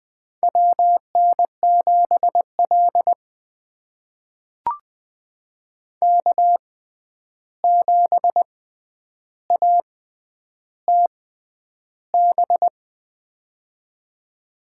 Morse code, heard as a keyed tone, about 20 wpm, spaced as for 10 wpm, 700 Hz. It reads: WN7L K7ATB
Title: USA Call Signs - 6x Character Spacing - 20wpm